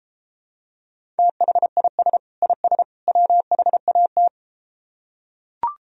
Morse code, heard as a keyed tone, about 33 wpm, 700 Hz. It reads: THIS IS WHAT